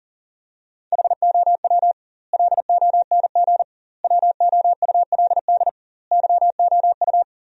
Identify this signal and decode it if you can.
Morse code, heard as a keyed tone, about 37 wpm, 700 Hz. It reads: HOW LONG WOULD YOU
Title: 2x Word-Spacing - Sentences from Top 500 Words - 40wpm